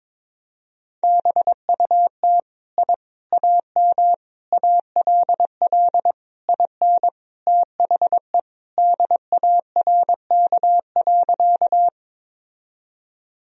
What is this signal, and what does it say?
Morse code, heard as a keyed tone, about 21 wpm, 700 Hz. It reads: BUT I AM ALL IN THE DARK.
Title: Sentences from Top 1000 Words - Review Part 2 - 22wpm